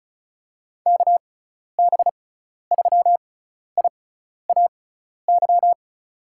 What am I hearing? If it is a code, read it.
Morse code, heard as a keyed tone, about 36 wpm, spaced as for 12 wpm, 690 Hz. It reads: KB3IAY
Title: USA Call Signs - 6 Characters - 6x Character Spacing - 35wpm